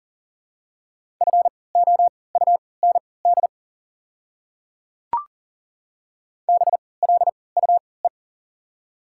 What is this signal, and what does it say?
Morse code, heard as a keyed tone, about 40 wpm, spaced as for 23 wpm, 700 Hz. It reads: FOUND BLUE
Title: 3x Character Spacing - Top 500 Words - 40wpm